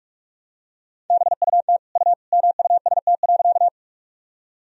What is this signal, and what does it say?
Morse code, heard as a keyed tone, about 46 wpm, 700 Hz. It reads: BUT U MUST.
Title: Sentences from Top 200 Words and Abbreviations 45wpm